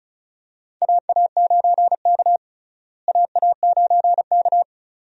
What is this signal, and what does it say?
Morse code, heard as a keyed tone, about 34 wpm, 700 Hz. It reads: AA9K AA9K
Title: USA Call Signs - 4 Characters - Repeated 2x - 35wpm